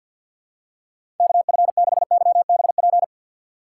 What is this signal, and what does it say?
Morse code, heard as a keyed tone, about 46 wpm, 700 Hz. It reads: KF6YBP
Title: USA Call Signs - 3x Character Spacing - 50wpm